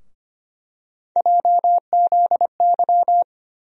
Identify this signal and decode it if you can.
Morse code, heard as a keyed tone, about 25 wpm, 700 Hz. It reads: JZY
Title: USA Call Sign Postfixes - 3 Letters 25wpm